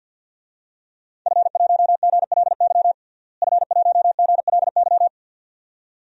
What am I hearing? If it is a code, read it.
Morse code, heard as a keyed tone, about 47 wpm, 700 Hz. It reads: F1GLY F1GLY
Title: DXCC-100 Call Signs - 5 Characters - Repeated 2x - 50wpm